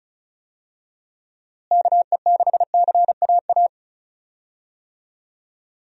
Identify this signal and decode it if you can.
Morse code, heard as a keyed tone, about 36 wpm, 700 Hz. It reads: KE6CAA
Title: USA Call Signs - 6 Characters - Repeated 3x - 35wpm